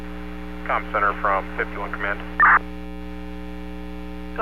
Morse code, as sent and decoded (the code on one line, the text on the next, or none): none